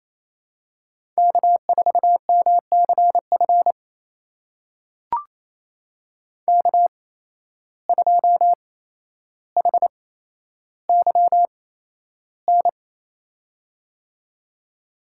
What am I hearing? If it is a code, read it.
Morse code, heard as a keyed tone, about 28 wpm, spaced as for 12 wpm, 700 Hz. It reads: K4MCF K2HYN
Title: USA Call Signs - 5 Characters - 8x Character Spacing - 28wpm